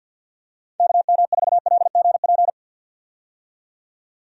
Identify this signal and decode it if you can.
Morse code, heard as a keyed tone, about 45 wpm, 700 Hz. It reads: KM4LGP
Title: USA Call Signs - 6 Characters - 50wpm